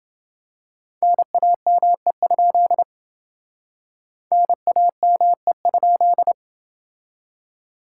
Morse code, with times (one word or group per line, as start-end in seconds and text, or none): 1.0-2.8 NAME?
4.3-6.3 NAME?